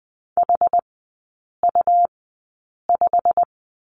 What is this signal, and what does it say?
Morse code, heard as a keyed tone, about 20 wpm, spaced as for 8 wpm, 700 Hz. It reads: HU5
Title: 127 Lesson 21e - Number 6 - Three Characters 20wpm